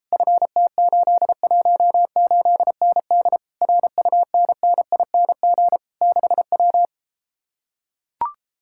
Morse code, heard as a keyed tone, about 33 wpm, 700 Hz. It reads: FT818ND RUNNING 6W